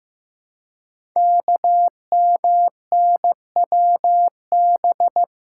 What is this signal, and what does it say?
Morse code, heard as a keyed tone, about 15 wpm, 700 Hz. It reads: KMNWB
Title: Code Groups - 5 Letters - 15wpm